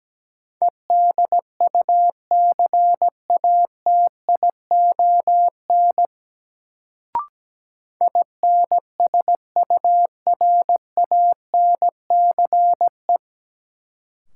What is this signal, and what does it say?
Morse code, heard as a keyed tone, about 17 wpm, 700 Hz. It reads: EDUCATION INSURANCE
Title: Sentences from Top 1100 Words - New Word Drill Alt - 17wpm